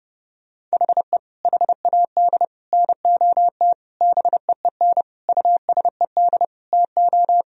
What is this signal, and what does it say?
Morse code, heard as a keyed tone, about 32 wpm, 700 Hz. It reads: HE HAD NOT BEEN USED TO
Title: Sentences from Top 1000 Words - Review Part 4 - 30wpm